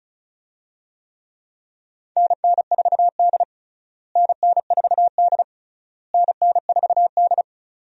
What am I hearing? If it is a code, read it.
Morse code, heard as a keyed tone, about 36 wpm, 700 Hz. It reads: NN4D NN4D NN4D
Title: USA Call Signs - 4 Characters - Repeated 3x - 35wpm